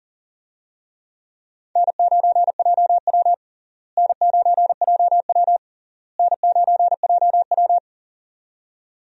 Morse code, heard as a keyed tone, about 40 wpm, 700 Hz. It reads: N9JW N9JW N9JW
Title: USA Call Signs - 4 Characters - Repeated 3x - 40wpm